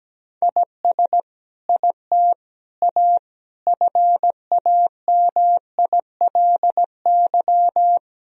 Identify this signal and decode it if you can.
Morse code, heard as a keyed tone, about 17 wpm, 700 Hz. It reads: IS IT A FAMILY
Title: Sentences from Top 1000 Words - Review Part 4 - 17wpm